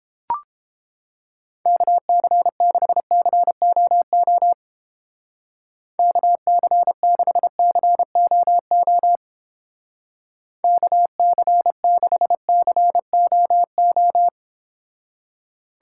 Morse code, heard as a keyed tone, about 29 wpm, 700 Hz. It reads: KC6COO KC6COO KC6COO